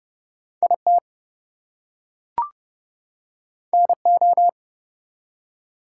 Morse code, heard as a keyed tone, about 31 wpm, 700 Hz. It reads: IT NO
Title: Two Letter Words - 30wpm